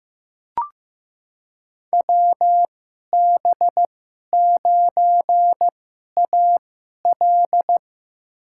Morse code, heard as a keyed tone, about 15 wpm, spaced as for 11 wpm, 700 Hz. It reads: WB9AL
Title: USA Call Signs - 5 Characters - 2x Character Spacing - 15wpm